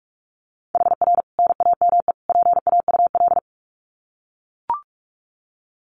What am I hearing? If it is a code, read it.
Morse code, heard as a keyed tone, about 44 wpm, 700 Hz. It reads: HR NAME PAUL
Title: Common QSO Phrases 45wpm